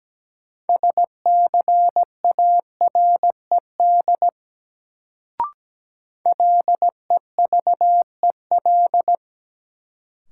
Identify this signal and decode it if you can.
Morse code, heard as a keyed tone, about 17 wpm, 700 Hz. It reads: SCARED LEVEL
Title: Sentences from Top 700 Words - New Word Drill - 17wpm